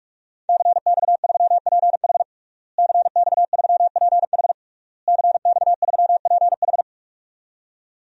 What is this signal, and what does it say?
Morse code, heard as a keyed tone, about 41 wpm, 700 Hz. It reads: CX3PH CX3PH CX3PH
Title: DXCC-100 Call Signs - 5 Characters - Repeated 3x - 45wpm